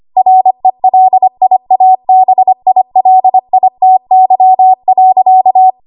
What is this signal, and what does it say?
Morse code, heard as a keyed tone, about 25 wpm, 750 Hz. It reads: RELIABILITY.
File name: Archive / 25 WPM / 25WPM